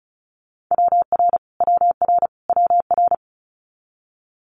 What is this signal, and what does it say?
Morse code, heard as a keyed tone, about 36 wpm, 700 Hz. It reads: WR WR WR
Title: Multi-letter Phonemes Repeated 3x 35wpm